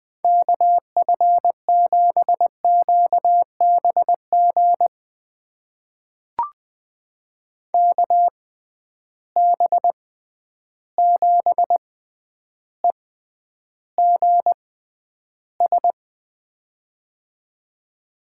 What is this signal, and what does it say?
Morse code, heard as a keyed tone, about 21 wpm, spaced as for 10 wpm, 700 Hz. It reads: KF7QBG KB7EGS
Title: USA Call Signs - 6 Characters - 6x Character Spacing - 20wpm